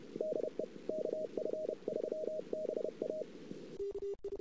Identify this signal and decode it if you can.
Morse code, heard as a keyed tone, about 31 wpm, 620 Hz, with noise. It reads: DE KF3BA
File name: CW / 2008-ss-cw-k5tr-op-n5rz -23:00